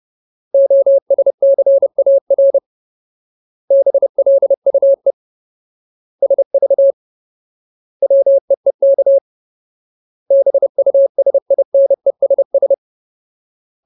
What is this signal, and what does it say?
Morse code, heard as a keyed tone, about 30 wpm, 550 Hz. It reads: OSCAR BLUE SV WEEK BUSINESS
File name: Ultimate Word List / 30WPM Ultimate